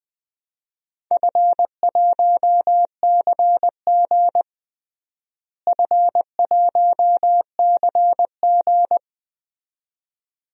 Morse code, heard as a keyed tone, about 20 wpm, 700 Hz. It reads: F1CG F1CG
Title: DXCC-100 Call Signs - 4 Characters - Repeated 2x - 20wpm